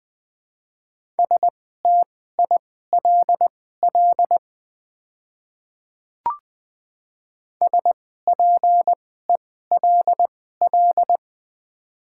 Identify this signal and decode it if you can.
Morse code, heard as a keyed tone, about 20 wpm, spaced as for 14 wpm, 700 Hz. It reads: STILL SPELL
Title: 2x Character Spacing - Top 300 Words - 20wpm